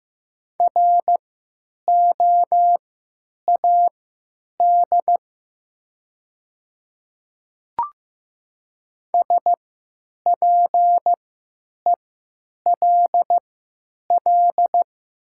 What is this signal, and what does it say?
Morse code, heard as a keyed tone, about 15 wpm, spaced as for 9 wpm, 700 Hz. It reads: ROAD SPELL
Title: 3x Character Spacing - Top 500 Words - 15wpm